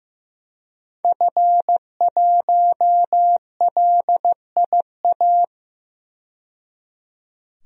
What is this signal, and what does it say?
Morse code, heard as a keyed tone, about 15 wpm, 700 Hz. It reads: F1LIA
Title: DXCC-100 Call Signs - 5 Characters - 15wpm